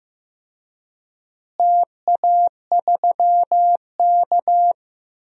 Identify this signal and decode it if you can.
Morse code, heard as a keyed tone, about 15 wpm, 700 Hz. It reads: TA3K
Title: DXCC-100 Call Signs - 4 Characters - Repeated 3x - 15wpm